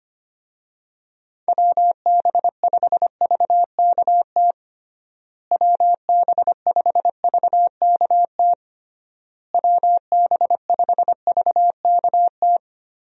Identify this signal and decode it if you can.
Morse code, heard as a keyed tone, about 25 wpm, 700 Hz. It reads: WB5VKT WB5VKT WB5VKT
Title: USA Call Signs - 6 Characters - Repeated 3x - 25wpm